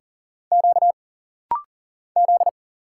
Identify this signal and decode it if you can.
Morse code, heard as a keyed tone, about 40 wpm, 700 Hz. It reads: Q Z
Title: Single Letter-Number - Rapid-Fire 40wpm